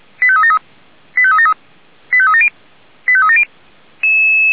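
Five-tone selective call (ZVEI), heard as quick runs of five tones, tones of about 70 ms. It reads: 85373 85373 85370 85370
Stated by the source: ZVEI five-tone selective call for 89235, received by SDR.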